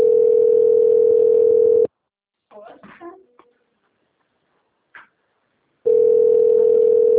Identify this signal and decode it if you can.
Ringback tone (North America), 440 + 480 Hz, with speech.
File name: out- - - - - .10913